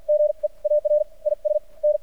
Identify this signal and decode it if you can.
Morse code, heard as a keyed tone, about 55 wpm, spaced as for 41 wpm, 600 Hz. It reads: GEAR IS N
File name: archived / CRSnet